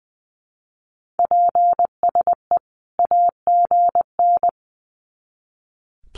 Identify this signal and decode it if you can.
Morse code, heard as a keyed tone, about 20 wpm, 700 Hz. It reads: PSE AGN
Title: Common QSO Phrases 20wpm